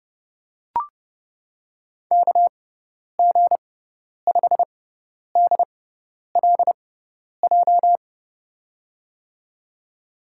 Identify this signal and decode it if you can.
Morse code, heard as a keyed tone, about 30 wpm, spaced as for 10 wpm, 700 Hz. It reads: KG5DLJ
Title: USA Call Signs - 6x Character Spacing - 30wpm